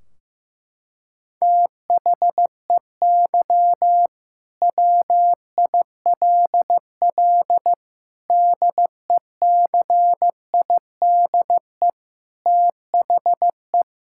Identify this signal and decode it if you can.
Morse code, heard as a keyed tone, about 15 wpm, 700 Hz. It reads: THEY WILL DECIDE THE
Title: Sentences from Top 1600 Words 15wpm